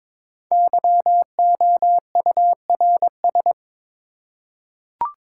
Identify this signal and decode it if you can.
Morse code, heard as a keyed tone, about 22 wpm, 690 Hz. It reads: YOURS
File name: 5 Letter Words - First 3 Letters Given - 22wpm